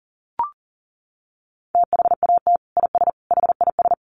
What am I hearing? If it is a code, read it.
Morse code, heard as a keyed tone, about 40 wpm, 700 Hz. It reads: THAT IS HIS
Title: Sentences from Top 900 Words 40wpm